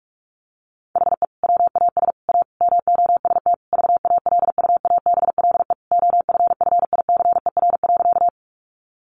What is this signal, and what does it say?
Morse code, heard as a keyed tone, about 44 wpm, 700 Hz. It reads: HE WAS A MOST VALUABLE OFFICER.
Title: Sentences from Top 1200 Words 45wpm